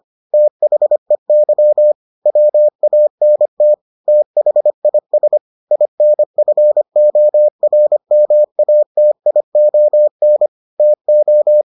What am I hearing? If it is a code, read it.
Morse code, heard as a keyed tone, about 25 wpm, 600 Hz. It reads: THEY WANT THIS INFORMATION TO